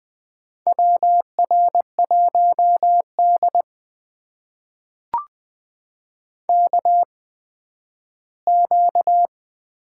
Morse code, heard as a keyed tone, about 20 wpm, spaced as for 12 wpm, 700 Hz. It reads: WR1D KQ
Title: USA Call Signs - 4 Characters - 8x Character Spacing - 20wpm